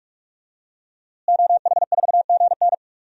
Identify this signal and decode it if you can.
Morse code, heard as a keyed tone, about 46 wpm, 700 Hz. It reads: OH4GN
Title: DXCC-100 Call Signs - 5 Characters - Repeated 3x - 45wpm